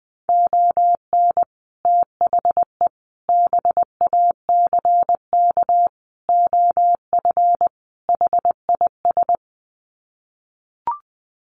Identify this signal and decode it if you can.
Morse code, heard as a keyed tone, about 20 wpm, 700 Hz. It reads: ON THE BACK OF HIS
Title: Sets of 5 Words Encore 20wpm